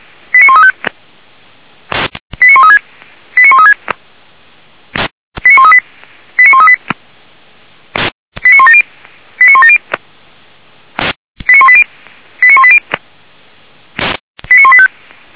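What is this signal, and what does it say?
Five-tone selective call (ZVEI), heard as quick runs of five tones, tones of about 70 ms. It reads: 89135 89136 89136 89138 89138 89170 89170 89180 89180 89185
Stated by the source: Five-tone ZVEI selective call received on an SDR.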